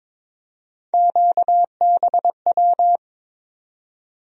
Morse code, heard as a keyed tone, about 22 wpm, 700 Hz. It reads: QBW
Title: USA Call Sign Postfixes - 3 Letters 22wpm